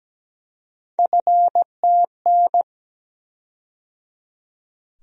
Morse code, heard as a keyed tone, about 17 wpm, 700 Hz. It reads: FTN
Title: USA Call Sign Postfixes - 3 Letters 17wpm